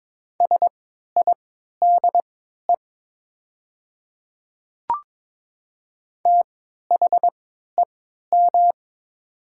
Morse code, heard as a keyed tone, about 22 wpm, spaced as for 12 wpm, 700 Hz. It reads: SIDE THEM
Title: 3x Character Spacing - Top 300 Words - 22wpm